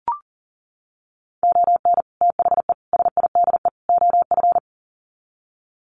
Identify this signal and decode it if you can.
Morse code, heard as a keyed tone, about 37 wpm, 700 Hz. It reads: ON THE SIDE OF
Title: Sets of 4 Words 40wpm